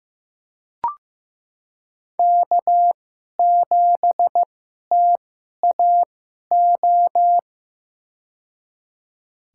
Morse code, heard as a keyed tone, about 15 wpm, spaced as for 11 wpm, 700 Hz. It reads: K7TAO
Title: USA Call Signs - 5 Characters - 2x Character Spacing - 15wpm